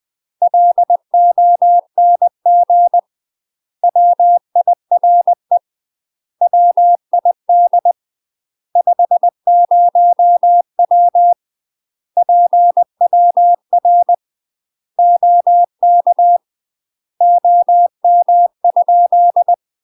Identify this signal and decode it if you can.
Morse code, heard as a keyed tone, about 20 wpm, 700 Hz. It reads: LONG WIRE WID 50W PWR OK OM?
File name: UK MASTER QSOs TRAINING / 20WPM / UK MASTER QSOs 0062